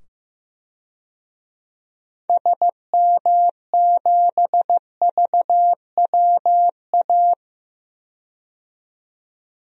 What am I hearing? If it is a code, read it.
Morse code, heard as a keyed tone, about 15 wpm, 700 Hz. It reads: SM7VWA
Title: DXCC-100 Call Signs - 6 Characters - 15wpm